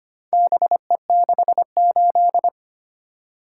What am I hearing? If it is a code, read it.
Morse code, heard as a keyed tone, about 25 wpm, 700 Hz. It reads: BE68
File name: practice / 25wpm / Four Letter-Number Combinations - 25wpm